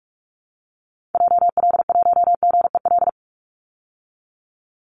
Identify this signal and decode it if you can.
Morse code, heard as a keyed tone, about 44 wpm, 700 Hz. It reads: JL1GEL
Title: Call Signs - 45wpm